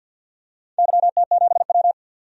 Morse code, heard as a keyed tone, about 51 wpm, 700 Hz. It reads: YT7W
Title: DXCC-100 Call Signs - 4 Characters - 50wpm